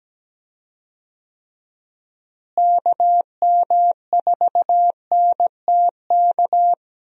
Morse code, heard as a keyed tone, about 17 wpm, 700 Hz. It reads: KM4NTK
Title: USA Call Signs - 6 Characters - Repeated 2x - 17wpm